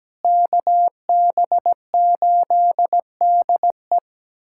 Morse code, heard as a keyed tone, about 17 wpm, 700 Hz. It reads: KB8DE
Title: USA Callsign DE Callsign - 5 Characters - 17wpm